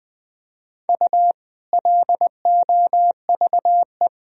Morse code, heard as a keyed tone, about 20 wpm, 700 Hz. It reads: U LOVE
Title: Sentences from Top 500 Words and Abbreviations 20wpm